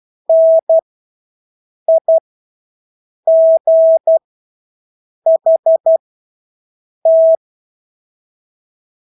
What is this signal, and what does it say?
Morse code, heard as a keyed tone, about 12 wpm, spaced as for 6 wpm, 650 Hz. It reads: NIGHT
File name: Engelske ord 12-6 wpm